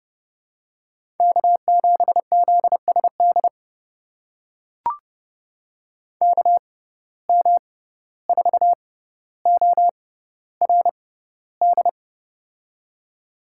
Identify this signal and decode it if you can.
Morse code, heard as a keyed tone, about 30 wpm, spaced as for 15 wpm, 700 Hz. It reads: K7ZSD KM4ORD
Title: USA Call Signs - 6x Character Spacing - 30wpm